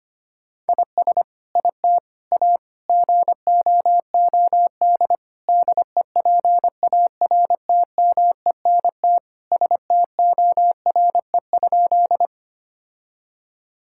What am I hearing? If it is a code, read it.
Morse code, heard as a keyed tone, about 25 wpm, 700 Hz. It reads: IS IT A GOOD DEPARTMENT STORE?
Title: Sentences from Top 1000 Words - Review Part 5 - 25wpm